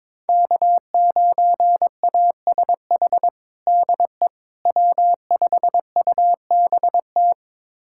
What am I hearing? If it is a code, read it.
Morse code, heard as a keyed tone, about 22 wpm, 700 Hz. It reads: K9ASH DE W5UBT
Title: USA Callsign DE Callsign - 5 Characters - 22wpm